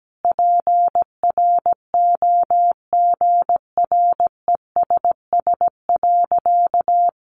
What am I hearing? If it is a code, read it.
Morse code, heard as a keyed tone, about 17 wpm, 700 Hz. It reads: PROGRESS.
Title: Sentences from Top 1400 Words 17wpm